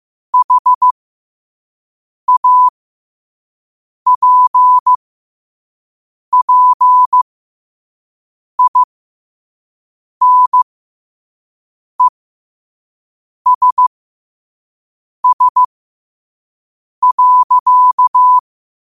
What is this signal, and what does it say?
Morse code, heard as a keyed tone, about 15 wpm, spaced as for 5 wpm, 1000 Hz. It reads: HAPPINESS.
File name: qotdfiles / 5wpm